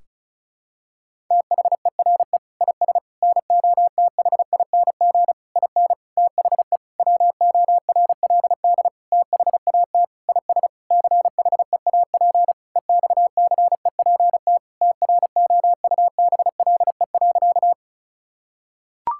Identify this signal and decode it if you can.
Morse code, heard as a keyed tone, about 34 wpm, 700 Hz. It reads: THERE IS NOTHING IN THE WORLD THAT IS CHEAP EXCEPT TROUBLE.
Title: Sentences from Top 1600 Words 35wpm